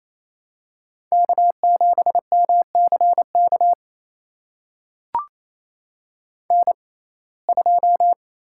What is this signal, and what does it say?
Morse code, heard as a keyed tone, about 28 wpm, spaced as for 20 wpm, 700 Hz. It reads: K7MCK N2
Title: USA Call Signs - 5 Characters - 6x Character Spacing - 28wpm